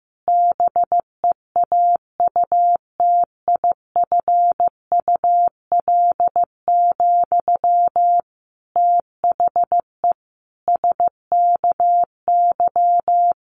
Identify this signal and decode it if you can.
Morse code, heard as a keyed tone, about 14 wpm, 700 Hz. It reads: BEAUTIFUL, THE SKY
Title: Sentences from Top 1200 Words 15wpm